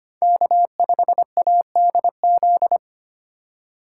Morse code, heard as a keyed tone, about 25 wpm, 700 Hz. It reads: K5ADZ
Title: USA Call Signs - 5 Characters - 3x Character Spacing - 25wpm